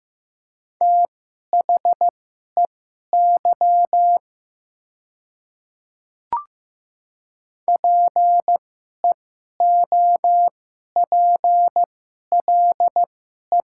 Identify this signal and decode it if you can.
Morse code, heard as a keyed tone, about 15 wpm, spaced as for 11 wpm, 700 Hz. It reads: THEY PEOPLE
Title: 2x Character Spacing - Top 500 Words - 15wpm